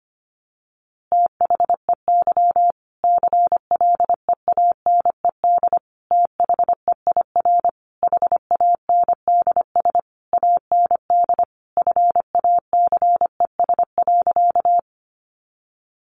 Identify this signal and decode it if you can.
Morse code, heard as a keyed tone, about 25 wpm, 700 Hz. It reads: THEY CLEANED THEIR HANDS AND FACES.
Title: A Sound Of Thunder 25wpm